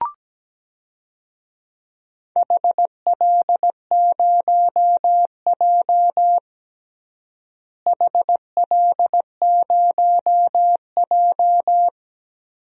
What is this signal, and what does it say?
Morse code, heard as a keyed tone, about 17 wpm, 700 Hz. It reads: HL0J HL0J
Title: DXCC-100 Call Signs - 4 Characters - Repeated 3x - 17wpm